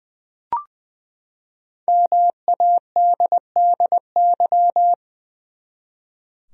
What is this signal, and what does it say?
Morse code, heard as a keyed tone, about 20 wpm, 700 Hz. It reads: MADDY
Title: USA Common Names - Max 5 Letters - 20wpm